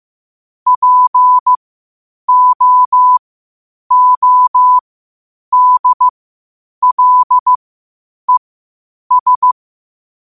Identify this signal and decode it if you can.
Morse code, heard as a keyed tone, about 15 wpm, spaced as for 8 wpm, 1000 Hz. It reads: POODLES
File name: qotdfiles / 8wpm